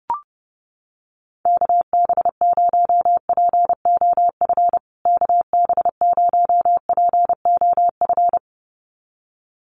Morse code, heard as a keyed tone, about 30 wpm, 700 Hz. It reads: KB0POF KB0POF
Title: USA Call Signs Repeated 2x 30wpm